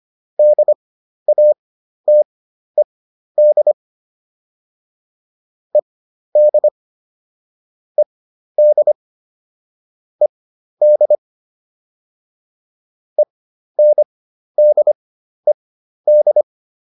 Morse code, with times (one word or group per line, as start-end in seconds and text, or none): 0.4-3.7 DATED
5.7-6.7 ED
8.0-8.9 ED
10.2-11.2 ED
13.2-16.4 ENDED